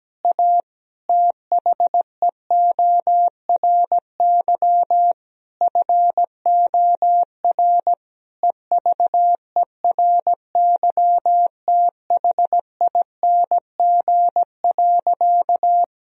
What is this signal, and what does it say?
Morse code, heard as a keyed tone, about 17 wpm, 700 Hz. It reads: A THEORY FOR EVERYTHING.
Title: Sentences from Top 1000 Words - Review Part 3 - 17wpm